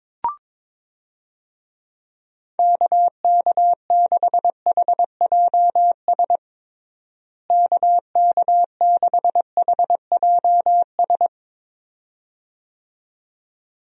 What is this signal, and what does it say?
Morse code, heard as a keyed tone, about 22 wpm, 700 Hz. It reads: KK6HJS KK6HJS
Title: USA Call Signs - 6 Characters - Repeated 2x - 22wpm